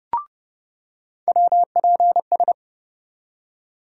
Morse code, heard as a keyed tone, about 30 wpm, 700 Hz. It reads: WPS
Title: USA Call Sign Postfixes - 3 Letters 30wpm